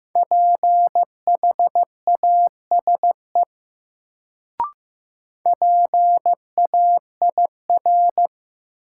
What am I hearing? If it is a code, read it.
Morse code, heard as a keyed tone, about 15 wpm, 700 Hz. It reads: PHASE PAIR
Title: Sentences from Top 1200 Words - New Word Drill - 15wpm